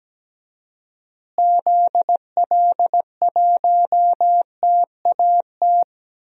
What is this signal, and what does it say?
Morse code, heard as a keyed tone, about 17 wpm, 700 Hz. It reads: ZL1TAT
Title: DXCC-100 Call Signs - 6 Characters - Repeated 3x - 17wpm